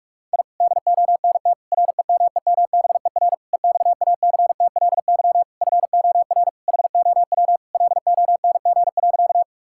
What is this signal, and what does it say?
Morse code, heard as a keyed tone, about 46 wpm, 700 Hz. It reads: I DONT REMEMBER EXACTLY FOR HOW LONG.
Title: Sentences from Top 1000 Words - Review Part 6 - 45wpm